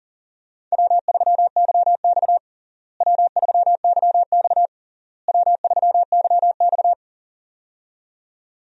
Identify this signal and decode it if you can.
Morse code, heard as a keyed tone, about 40 wpm, 700 Hz. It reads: W3YX W3YX W3YX
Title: USA Call Signs - 4 Characters - Repeated 3x - 40wpm